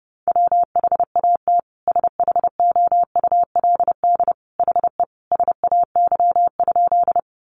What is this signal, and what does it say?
Morse code, heard as a keyed tone, about 30 wpm, 700 Hz. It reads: WHAT SHOULD HE SAY?
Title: Sentences from Top 300 Words 30wpm